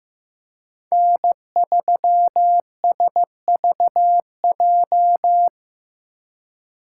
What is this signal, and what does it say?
Morse code, heard as a keyed tone, about 18 wpm, 700 Hz. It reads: N3SVJ
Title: Speed Racing - USA 1x3 Call Signs 15wpm